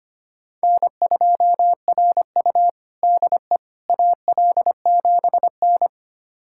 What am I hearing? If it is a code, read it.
Morse code, heard as a keyed tone, about 25 wpm, 700 Hz. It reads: N2RU DE AL7N
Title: USA Callsign DE Callsign - 4 Characters - 25wpm